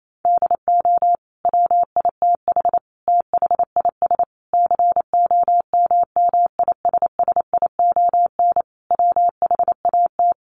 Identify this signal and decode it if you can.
Morse code, heard as a keyed tone, about 28 wpm, 700 Hz. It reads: DO WITH THIS COMMISSION WHAT
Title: Sentences from Top 1400 Words 28wpm